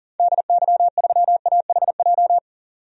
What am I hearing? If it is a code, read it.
Morse code, heard as a keyed tone, about 40 wpm, 700 Hz. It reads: DY3AHJ
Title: DXCC-100 Call Signs - 6 Characters - 40wpm